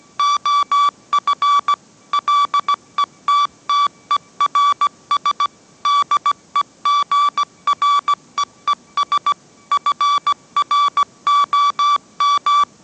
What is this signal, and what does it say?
Morse code, heard as a keyed tone, about 17 wpm, 1150 Hz. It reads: OF LETTERS DEGREES FROM